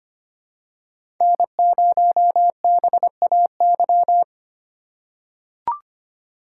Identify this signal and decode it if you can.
Morse code, heard as a keyed tone, about 25 wpm, 700 Hz. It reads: N0BAY
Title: USA Call Signs - 3x Character Spacing - 25wpm